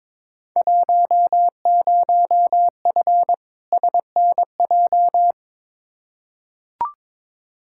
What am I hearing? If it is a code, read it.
Morse code, heard as a keyed tone, about 21 wpm, 700 Hz. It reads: 10F SNJ
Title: Field Day Exchanges Encore - 22wpm